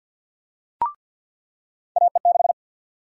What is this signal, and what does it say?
Morse code, heard as a keyed tone, about 53 wpm, 700 Hz. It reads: AE6
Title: USA Call Sign Prefixes - 50wpm